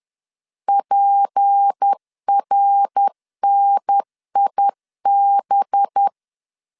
Morse code, heard as a keyed tone, about 10 wpm, 800 Hz. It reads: PRNIB
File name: LEZIONI / 6 PKX0 / 6 P 10wpm 2432